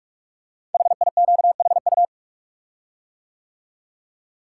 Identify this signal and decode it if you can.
Morse code, heard as a keyed tone, about 44 wpm, 690 Hz. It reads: HIQHU
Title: Code Groups - 5 Letters - 45wpm